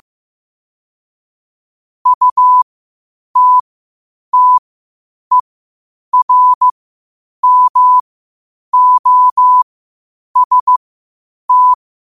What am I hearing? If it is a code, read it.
Morse code, heard as a keyed tone, about 15 wpm, spaced as for 8 wpm, 1000 Hz. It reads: UTTERMOST
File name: qotdfiles / 8wpm